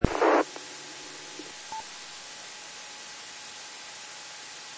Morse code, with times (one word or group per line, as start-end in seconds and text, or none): none